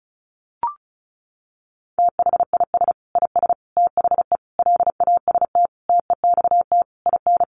Morse code, heard as a keyed tone, about 34 wpm, 700 Hz. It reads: THIS IS THE LAST TEXT IN
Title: Sentences from Top 1200 Words 35wpm